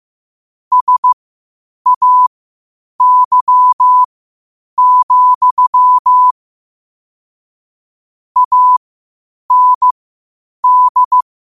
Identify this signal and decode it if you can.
Morse code, heard as a keyed tone, about 15 wpm, spaced as for 8 wpm, 1000 Hz. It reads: SAY, AND
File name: qotdfiles / 8wpm